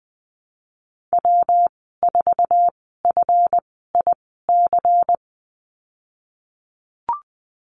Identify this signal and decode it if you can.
Morse code, heard as a keyed tone, about 20 wpm, spaced as for 15 wpm, 700 Hz. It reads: W4FIC
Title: 167 Lesson 27g - Number 7 - Call Signs 20wpm